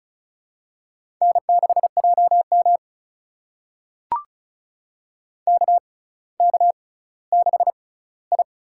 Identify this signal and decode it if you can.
Morse code, heard as a keyed tone, about 36 wpm, spaced as for 18 wpm, 700 Hz. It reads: N6JM KK6I
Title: USA Call Signs - 6x Character Spacing - 35wpm